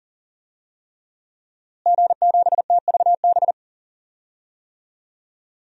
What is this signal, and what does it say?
Morse code, heard as a keyed tone, about 40 wpm, 700 Hz. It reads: G7TVB
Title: DXCC-100 Call Signs - 5 Characters - Repeated 3x - 40wpm